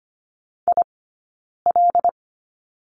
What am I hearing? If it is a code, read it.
Morse code, heard as a keyed tone, about 25 wpm, spaced as for 9 wpm, 690 Hz. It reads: IL
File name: Lesson 15d - Number 5 - Two Characters 25wpm